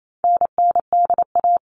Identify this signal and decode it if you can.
Morse code, heard as a keyed tone, about 28 wpm, 700 Hz. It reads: NNDA
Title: All Cut Numbers - 4-Digit 28wpm